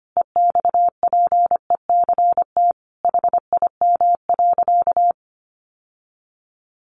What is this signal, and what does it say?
Morse code, heard as a keyed tone, about 25 wpm, 700 Hz. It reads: EXPECT HIM.
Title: Sentences from Top 1100 Words 25wpm